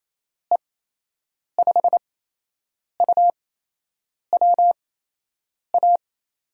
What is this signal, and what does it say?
Morse code, heard as a keyed tone, about 28 wpm, spaced as for 8 wpm, 700 Hz. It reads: E5UWA
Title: USA Call Signs - 6 Characters - 8x Character Spacing - 28wpm